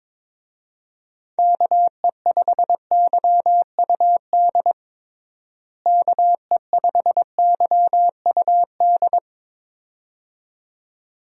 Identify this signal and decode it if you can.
Morse code, heard as a keyed tone, about 22 wpm, 700 Hz. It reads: KE5YUD KE5YUD
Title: USA Call Signs - 6 Characters - Repeated 2x - 22wpm